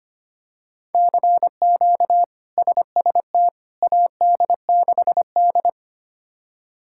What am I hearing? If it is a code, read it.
Morse code, heard as a keyed tone, about 25 wpm, 700 Hz. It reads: CQ SST AD6D
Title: K1USN SST QSO Practice - 25wpm